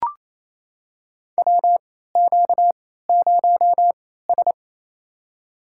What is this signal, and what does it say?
Morse code, heard as a keyed tone, about 28 wpm, spaced as for 16 wpm, 700 Hz. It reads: WQ0S
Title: USA Call Signs - 3x Character Spacing - 28wpm